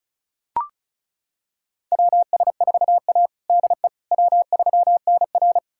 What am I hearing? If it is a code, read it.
Morse code, heard as a keyed tone, about 36 wpm, 700 Hz. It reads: WS4A DE W3NR
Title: USA Callsign DE Callsign - 4 Characters - 35wpm